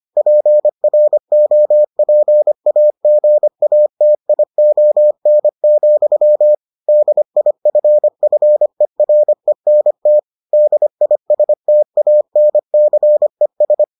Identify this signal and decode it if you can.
Morse code, heard as a keyed tone, about 25 wpm, 600 Hz. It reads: PROPAGATION, DIFFERENT DISTANCES